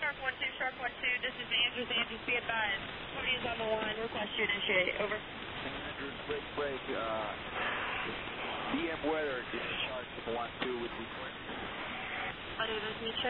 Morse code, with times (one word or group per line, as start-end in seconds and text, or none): none